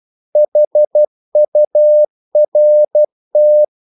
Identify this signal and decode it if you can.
Morse code, heard as a keyed tone, about 12 wpm, 600 Hz. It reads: HURT
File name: CW / Practice / If12